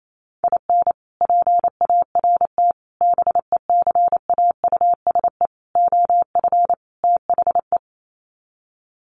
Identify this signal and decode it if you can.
Morse code, heard as a keyed tone, about 28 wpm, 700 Hz. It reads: IN PART BECAUSE OF THE
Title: Sets of 5 Words Encore2 28wpm